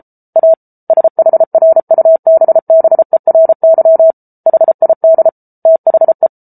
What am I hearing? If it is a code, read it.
Morse code, heard as a keyed tone, about 34 wpm, 650 Hz. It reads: A SHRUBBERY HID THE